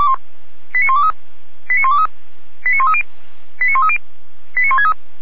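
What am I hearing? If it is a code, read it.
Five-tone selective call (ZVEI), heard as quick runs of five tones, tones of about 70 ms. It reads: 89123 89123 89130 89130 89163
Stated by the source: ZVEI five-tone call received by SDR.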